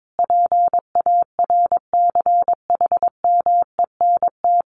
Morse code, heard as a keyed tone, about 22 wpm, 700 Hz. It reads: PARCHMENT
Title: A Sound Of Thunder 22wpm